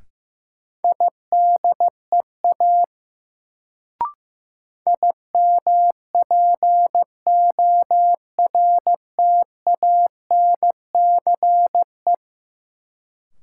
Morse code, heard as a keyed tone, about 15 wpm, 700 Hz. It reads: IDEA IMPORTANCE
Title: Sets of 3 Words - New Word Drill - 15wpm